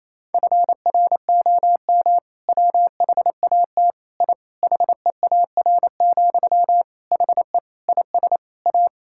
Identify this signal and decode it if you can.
Morse code, heard as a keyed tone, about 28 wpm, 700 Hz. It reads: FROM WHAT I HEAR, HE IS A
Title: Sentences from Top 1000 Words - Review Part 6 - 28wpm